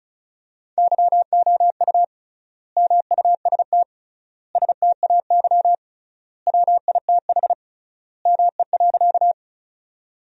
Morse code, heard as a keyed tone, about 36 wpm, 700 Hz. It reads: YOU MUST STAY WITH ME.
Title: 3x Word-Spacing - Sentences from Top 500 Words - 35wpm